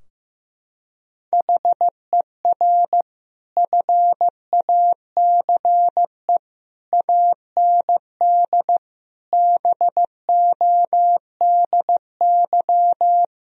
Morse code, heard as a keyed tone, about 15 wpm, 700 Hz. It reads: HER FACE AND BODY
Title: Sentences from Top 1000 Words - Review Part 2 - 15wpm